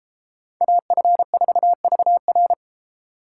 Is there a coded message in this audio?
Yes: Morse code from 0.6 to 2.5 s, heard as a keyed tone, about 32 wpm, 700 Hz. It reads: AF4VR